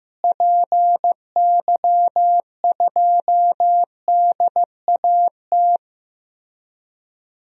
Morse code, heard as a keyed tone, about 15 wpm, 700 Hz. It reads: PY2DAT
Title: DXCC-100 Call Signs - 6 Characters - Repeated 3x - 15wpm